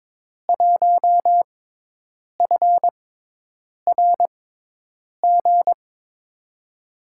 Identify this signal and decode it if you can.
Morse code, heard as a keyed tone, about 22 wpm, spaced as for 8 wpm, 700 Hz. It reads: 1FRG
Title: USA Call Signs - 6 Characters - 6x Character Spacing - 22wpm